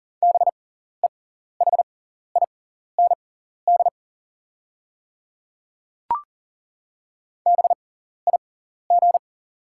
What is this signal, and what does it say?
Morse code, heard as a keyed tone, about 40 wpm, spaced as for 14 wpm, 700 Hz. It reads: BEHIND BIG
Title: 6x Character Spacing - Top 500 Words - 40wpm